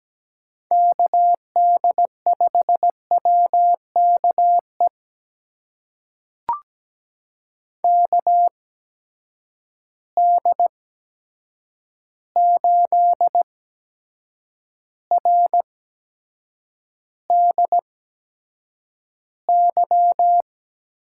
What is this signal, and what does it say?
Morse code, heard as a keyed tone, about 17 wpm, spaced as for 7 wpm, 700 Hz. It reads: KD5WKE KD8RDY
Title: USA Call Signs - 6 Characters - 8x Character Spacing - 17wpm